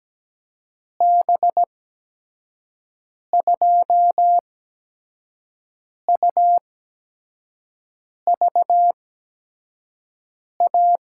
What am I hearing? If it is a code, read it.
Morse code, heard as a keyed tone, about 17 wpm, spaced as for 5 wpm, 700 Hz. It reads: B2UVA
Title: USA Call Signs - 8x Character Spacing - 17wpm